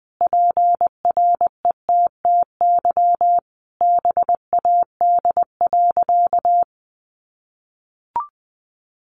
Morse code, heard as a keyed tone, about 19 wpm, 700 Hz. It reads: PRETTY BAD.
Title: Sentences from Top 1000 Words 20wpm